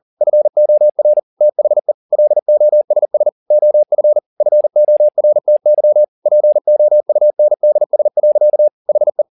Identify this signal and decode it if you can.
Morse code, heard as a keyed tone, about 40 wpm, 600 Hz. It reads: FOR THE LOSS OF FORTY POUNDS. HE
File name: the old man and the sea 2 40wpm